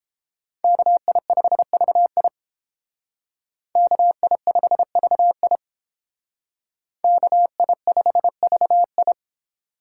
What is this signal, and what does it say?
Morse code, heard as a keyed tone, about 29 wpm, 700 Hz. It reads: KI5VI KI5VI KI5VI